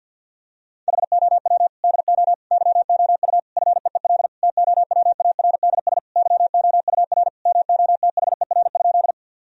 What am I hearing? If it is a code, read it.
Morse code, heard as a keyed tone, about 53 wpm, 700 Hz. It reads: HOW DO YOU FEEL TOWARDS YOUR MOTHER?
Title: Sentences from Top 1000 Words - Review Part 2 - 50wpm